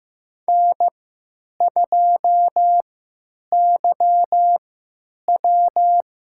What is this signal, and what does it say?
Morse code, heard as a keyed tone, about 15 wpm, spaced as for 9 wpm, 700 Hz. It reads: N2YW
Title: USA Call Signs - 4 Characters - 3x Character Spacing - 15wpm